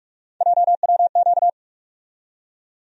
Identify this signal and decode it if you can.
Morse code, heard as a keyed tone, about 46 wpm, 700 Hz. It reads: JWQ
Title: USA Call Sign Postfixes - 3 Letters 45wpm